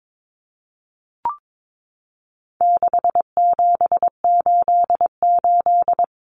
Morse code, heard as a keyed tone, about 22 wpm, 700 Hz. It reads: 6788
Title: Numbers - 4-Digit 22wpm